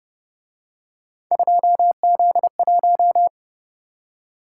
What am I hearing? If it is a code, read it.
Morse code, heard as a keyed tone, about 31 wpm, 700 Hz. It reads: 2Z1